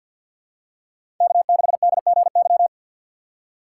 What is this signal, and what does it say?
Morse code, heard as a keyed tone, about 53 wpm, 690 Hz. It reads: K6DGY
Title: USA Call Signs - 5 Characters - 4x Character Spacing - 50wpm